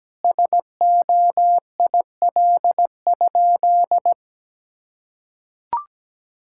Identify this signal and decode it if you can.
Morse code, heard as a keyed tone, about 17 wpm, 700 Hz. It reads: SOIL?
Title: Sentences from Top 1600 Words 17wpm